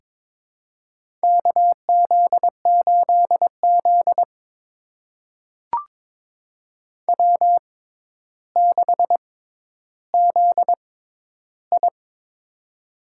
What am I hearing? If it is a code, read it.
Morse code, heard as a keyed tone, about 22 wpm, spaced as for 11 wpm, 700 Hz. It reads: KZ8Z W6ZI
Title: USA Call Signs - 4 Characters - 6x Character Spacing - 22wpm